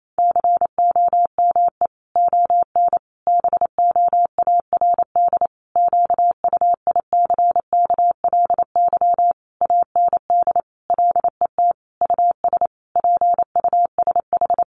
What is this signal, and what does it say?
Morse code, heard as a keyed tone, about 28 wpm, 700 Hz. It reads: COME ON BOARD QUICKLY AND LET US PUSH
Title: Sentences from Top 1300 Words 28wpm